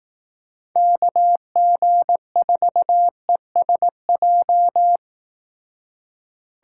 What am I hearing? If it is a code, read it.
Morse code, heard as a keyed tone, about 20 wpm, 700 Hz. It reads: KG4ESJ